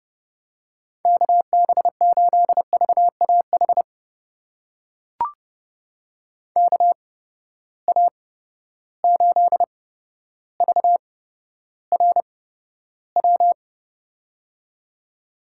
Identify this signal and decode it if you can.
Morse code, heard as a keyed tone, about 31 wpm, spaced as for 13 wpm, 700 Hz. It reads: KB8VAH KA8VRW
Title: USA Call Signs - 6 Characters - 8x Character Spacing - 30wpm